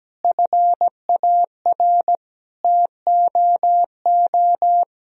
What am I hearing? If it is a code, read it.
Morse code, heard as a keyed tone, about 17 wpm, 700 Hz. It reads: FAR TOO
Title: Sentences from Top 1000 Words - Review Part 1 - 17wpm